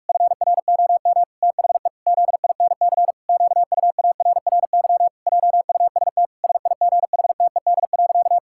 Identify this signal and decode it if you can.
Morse code, heard as a keyed tone, about 46 wpm, 700 Hz. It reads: FROM THE ZINC QUARRY JUST SIGHTED.
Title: Pangram Challenge v2 - 45wpm